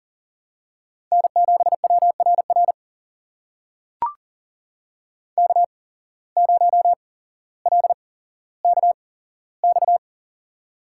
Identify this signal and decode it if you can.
Morse code, heard as a keyed tone, about 40 wpm, spaced as for 17 wpm, 700 Hz. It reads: N7WRR K0LKX
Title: USA Call Signs - 5 Characters - 8x Character Spacing - 40wpm